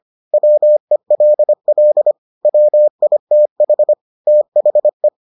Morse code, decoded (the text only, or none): WELL WITH THE